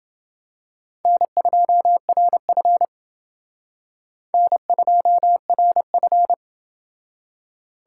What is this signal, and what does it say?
Morse code, heard as a keyed tone, about 28 wpm, 700 Hz. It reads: N2RF N2RF